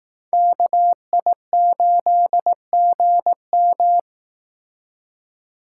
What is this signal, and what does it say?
Morse code, heard as a keyed tone, about 18 wpm, 700 Hz. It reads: KI8GM
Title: Speed Racing - USA Call Signs 15wpm